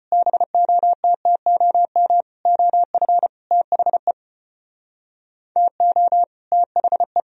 Morse code, heard as a keyed tone, about 35 wpm, 700 Hz. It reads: BOTTOM OF THE TO THE